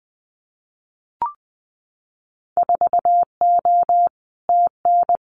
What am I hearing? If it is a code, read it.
Morse code, heard as a keyed tone, about 20 wpm, 700 Hz. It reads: 4O TN
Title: Winter Field Day Exchanges Encore 20wpm